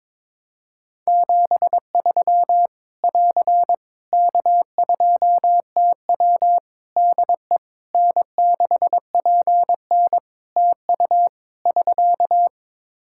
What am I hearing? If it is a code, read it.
Morse code, heard as a keyed tone, about 22 wpm, 700 Hz. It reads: 73 <AR> K2TW DE N6PN TU <SK>
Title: Practice QSO - Exchange 4 - Ending - 22wpm